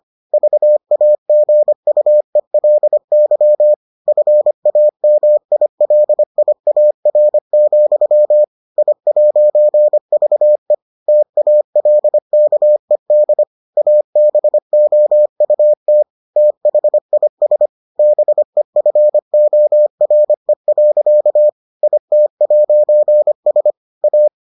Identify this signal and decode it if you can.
Morse code, heard as a keyed tone, about 25 wpm, 600 Hz. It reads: VAGUELY FAMILIAR, I'VE TALKED ABOUT THIS BEFORE. IT'S A